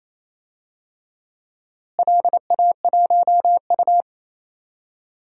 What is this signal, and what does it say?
Morse code, heard as a keyed tone, about 28 wpm, 700 Hz. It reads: LA1U
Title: DXCC-100 Call Signs - 4 Characters - Repeated 3x - 28wpm